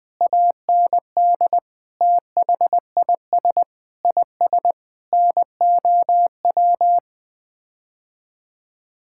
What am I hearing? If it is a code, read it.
Morse code, heard as a keyed tone, about 20 wpm, 700 Hz. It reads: AND THIS IS NOW
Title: Sets of 8 Words - 20wpm